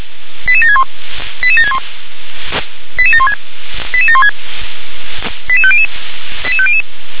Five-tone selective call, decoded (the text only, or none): 88761 88761 88716 88716 88500 88500